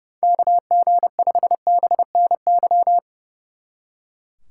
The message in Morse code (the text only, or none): KG5BNY